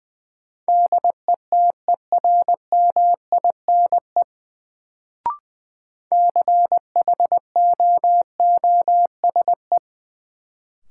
Morse code, heard as a keyed tone, about 20 wpm, 700 Hz. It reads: DETERMINE CHOOSE
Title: Sentences from Top 700 Words - New Word Drill - 20wpm